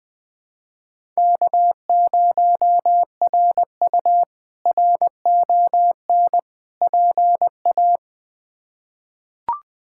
Morse code, heard as a keyed tone, about 21 wpm, 700 Hz. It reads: K0RU RON PA
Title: K1USN SST QSO Practice - 20wpm